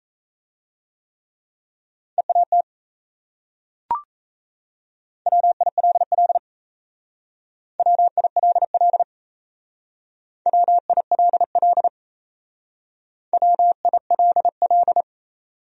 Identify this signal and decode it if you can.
Morse code, heard as a keyed tone, about 35 wpm, 700 Hz. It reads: EAT WILL WILL WILL WILL
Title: Speed-Racing Top 300 Words 28wpm